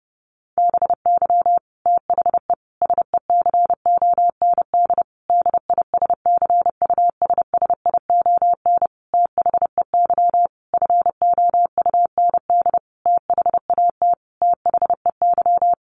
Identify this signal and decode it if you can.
Morse code, heard as a keyed tone, about 30 wpm, 700 Hz. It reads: BY THE SECOND DISCUSSION THEY FOUND THAT THEY
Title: Sentences from Top 1200 Words 30wpm